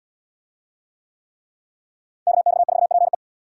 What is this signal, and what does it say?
Morse code, heard as a keyed tone, about 73 wpm, 700 Hz. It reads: KX4PE